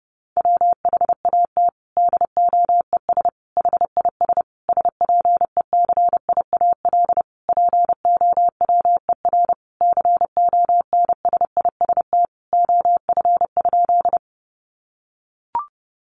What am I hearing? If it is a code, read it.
Morse code, heard as a keyed tone, about 30 wpm, 700 Hz. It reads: WHAT DOES HIS SPECIAL POWER CONSIST OF?
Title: Sentences from Top 1400 Words 30wpm